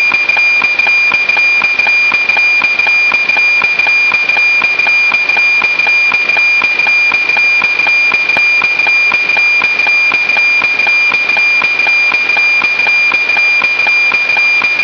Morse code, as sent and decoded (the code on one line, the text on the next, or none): none